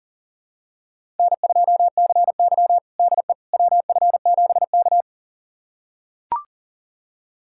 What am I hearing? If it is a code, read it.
Morse code, heard as a keyed tone, about 40 wpm, 700 Hz. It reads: N2CY DE WF7K
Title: USA Callsign DE Callsign - 4 Characters - 40wpm